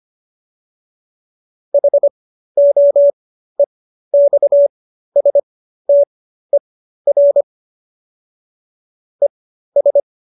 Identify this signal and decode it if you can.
Morse code, heard as a keyed tone, about 25 wpm, spaced as for 13 wpm, 570 Hz. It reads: HOEXSTER ES